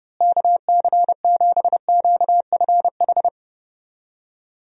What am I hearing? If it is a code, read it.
Morse code, heard as a keyed tone, about 30 wpm, 700 Hz. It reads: KC7QFH
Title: USA Call Signs - 6 Characters - Repeated 3x - 30wpm